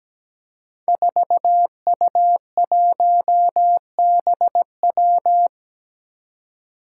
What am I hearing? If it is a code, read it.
Morse code, heard as a keyed tone, about 17 wpm, 700 Hz. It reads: 4U1BW
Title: Five Challenging Letter-Number Combinations 17wpm